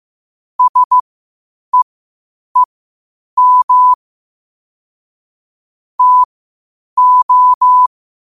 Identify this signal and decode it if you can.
Morse code, heard as a keyed tone, about 15 wpm, spaced as for 8 wpm, 1000 Hz. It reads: SEEM TO
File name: qotdfiles / 8wpm